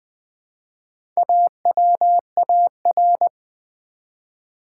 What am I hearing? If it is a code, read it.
Morse code, heard as a keyed tone, about 20 wpm, 700 Hz. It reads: AWAR
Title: Build A Word - 1 Letter At A Time - 5-Letter Words - 20wpm